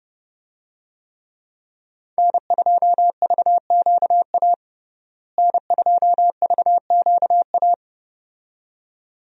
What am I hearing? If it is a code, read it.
Morse code, heard as a keyed tone, about 31 wpm, 700 Hz. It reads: N2VQA N2VQA
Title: USA Call Signs - 5 Characters - Repeated 2x - 30wpm